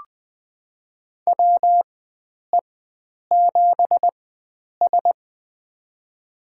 Morse code, heard as a keyed tone, about 20 wpm, spaced as for 9 wpm, 700 Hz. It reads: WE7S
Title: USA Call Signs - 4 Characters - 4x Character Spacing - 20wpm